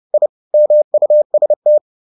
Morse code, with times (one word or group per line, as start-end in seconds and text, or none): 0.1-0.3 I
0.5-1.8 MUST